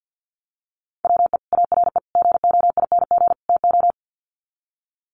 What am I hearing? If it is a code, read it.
Morse code, heard as a keyed tone, about 52 wpm, 700 Hz. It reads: WE ARE GOING TO
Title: Sets of 4 Words 50wpm